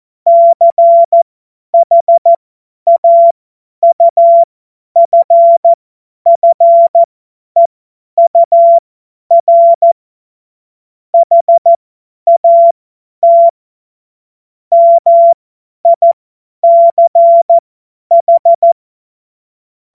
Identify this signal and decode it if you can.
Morse code, heard as a keyed tone, about 14 wpm, spaced as for 10 wpm, 680 Hz. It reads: CHAUFFEUR HAT MICH
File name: Morsekurs / Witze / Witz 06. Dezember 2022 50 F70